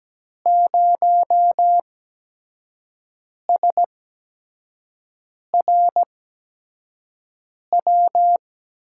Morse code, heard as a keyed tone, about 17 wpm, spaced as for 5 wpm, 700 Hz. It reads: 0SRW
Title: USA Call Signs - 5 Characters - 8x Character Spacing - 17wpm